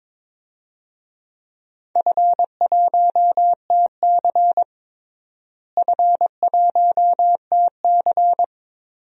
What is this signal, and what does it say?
Morse code, heard as a keyed tone, about 21 wpm, 700 Hz. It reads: F1TC F1TC
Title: DXCC-100 Call Signs - 4 Characters - Repeated 2x - 22wpm